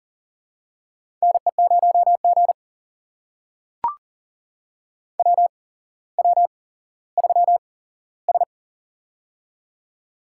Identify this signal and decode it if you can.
Morse code, heard as a keyed tone, about 42 wpm, spaced as for 17 wpm, 700 Hz. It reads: NE0G WW3S
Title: USA Call Signs - 4 Characters - 8x Character Spacing - 40wpm